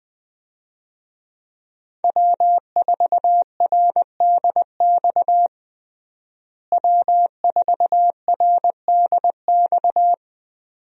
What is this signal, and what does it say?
Morse code, heard as a keyed tone, about 20 wpm, 700 Hz. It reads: W4RDX W4RDX
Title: USA Call Signs - 5 Characters - Repeated 2x - 20wpm